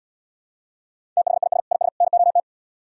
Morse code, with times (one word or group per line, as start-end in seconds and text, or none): 1.2-1.6 THIS
1.7-1.9 IS
2.0-2.4 NOT